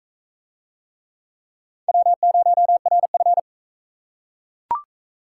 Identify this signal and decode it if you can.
Morse code, heard as a keyed tone, about 35 wpm, 700 Hz. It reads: W0RF